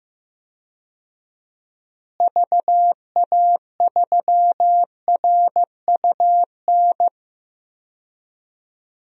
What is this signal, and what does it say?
Morse code, heard as a keyed tone, about 15 wpm, 700 Hz. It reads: VA3RUN
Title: DXCC-100 Call Signs - 6 Characters - Repeated 3x - 15wpm